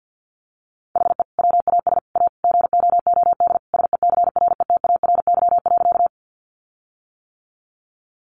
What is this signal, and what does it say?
Morse code, heard as a keyed tone, about 50 wpm, 700 Hz. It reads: HE WAS A GOOD SECRETARY.
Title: Sentences from Top 1300 Words 50wpm